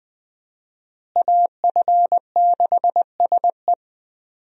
Morse code, heard as a keyed tone, about 20 wpm, 700 Hz. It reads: AF6SE
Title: USA Call Signs - 5 Characters - Repeated 3x - 20wpm